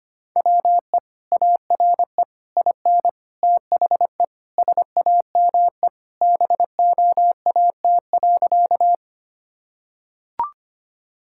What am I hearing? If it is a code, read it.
Morse code, heard as a keyed tone, about 25 wpm, 700 Hz. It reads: WE ARE IN THE SAME BOAT.
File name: practice / 25wpm / Sentences with 3-Word Phrases - 25wpm